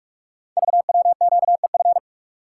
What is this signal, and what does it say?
Morse code, heard as a keyed tone, about 44 wpm, 690 Hz. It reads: VWQEF
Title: Code Groups - 5 Letters - 45wpm